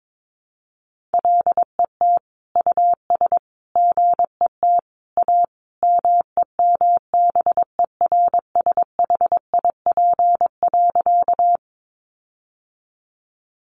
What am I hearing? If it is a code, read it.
Morse code, heard as a keyed tone, about 22 wpm, 700 Hz. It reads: LET US GET A MEMBERSHIP.
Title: Sentences from Top 1400 Words 22wpm